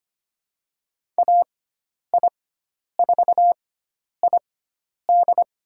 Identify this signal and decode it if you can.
Morse code, heard as a keyed tone, about 25 wpm, spaced as for 10 wpm, 700 Hz. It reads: AI4ID
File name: Lesson 16g - Letter c - Call Signs 25wpm